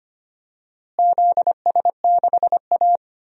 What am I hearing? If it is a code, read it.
Morse code, heard as a keyed tone, about 25 wpm, 700 Hz. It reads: ZS6A
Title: DXCC-100 Call Signs - 4 Characters - Repeated 3x - 25wpm